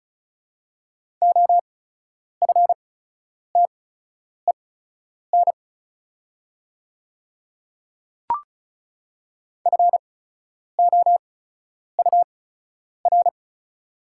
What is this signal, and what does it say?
Morse code, heard as a keyed tone, about 34 wpm, spaced as for 10 wpm, 700 Hz. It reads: OFTEN FOUR
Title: 8x Character Spacing - Top 300 Words - 35wpm